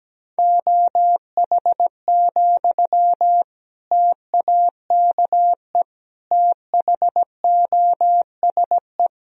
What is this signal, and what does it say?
Morse code, heard as a keyed tone, about 17 wpm, 700 Hz. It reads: OH, TAKE THOSE
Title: Sentences from Top 1000 Words - Review Part 1 - 17wpm